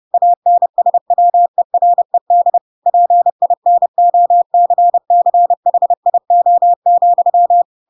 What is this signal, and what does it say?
Morse code, heard as a keyed tone, about 30 wpm, 700 Hz. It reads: ANSWERED PINOCCHIO,
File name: Pinocchio / Pinocchio ch8 30 30 ik1hge